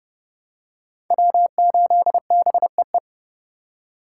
Morse code, heard as a keyed tone, about 30 wpm, 700 Hz. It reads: W8BEE